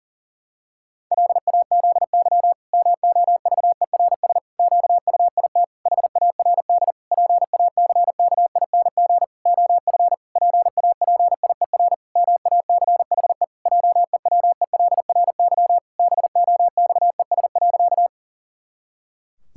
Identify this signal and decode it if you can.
Morse code, heard as a keyed tone, about 37 wpm, 700 Hz. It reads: LAZY MOVERS QUIT HARD PACKING OF PAPIER MACHE JEWELRY BOXES.
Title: Pangram Challenge v2 - 40wpm